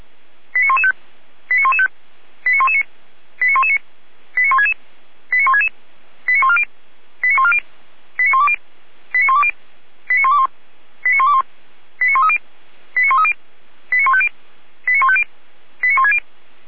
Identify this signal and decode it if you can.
Five-tone selective call (ZVEI), heard as quick runs of five tones, tones of about 80 ms. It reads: 89115 89115 89119 89119 89166 89166 89140 89140 89120 89120 89121 89121 89130 89130 89150 89150 89160